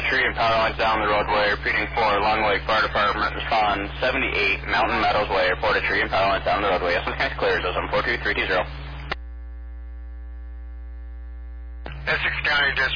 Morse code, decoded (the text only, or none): none